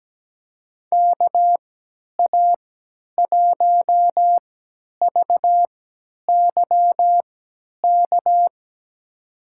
Morse code, heard as a keyed tone, about 17 wpm, spaced as for 10 wpm, 700 Hz. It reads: KA1VYK
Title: USA Call Signs - 3x Character Spacing - 17wpm